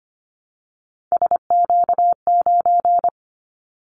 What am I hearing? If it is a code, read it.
Morse code, heard as a keyed tone, about 25 wpm, 700 Hz. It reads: SQ9
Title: Callsign Prefixes for Top 25 DXCC - Repeated 3x - 25wpm